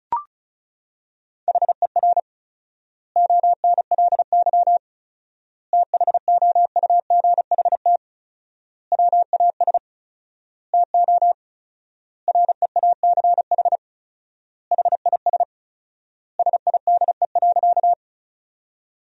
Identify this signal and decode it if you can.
Morse code, heard as a keyed tone, about 34 wpm, 700 Hz. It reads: HER ONLY THOUGHT WAS TO REACH HIS SIDE.
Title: 4x Word-Spacing - Sentences from Top 400 Words - 35wpm